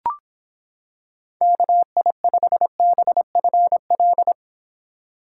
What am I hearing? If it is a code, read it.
Morse code, heard as a keyed tone, about 25 wpm, 700 Hz. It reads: KI5BFL